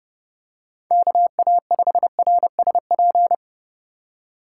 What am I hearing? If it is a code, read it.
Morse code, heard as a keyed tone, about 28 wpm, 700 Hz. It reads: KA5RSP